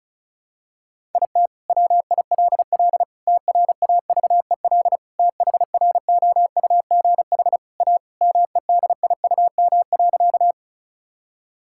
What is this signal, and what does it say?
Morse code, heard as a keyed tone, about 36 wpm, 700 Hz. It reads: IT WILL TRAVEL THROUGH A MEDIUM.
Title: Sentences from Top 1000 Words - Review Part 6 - 35wpm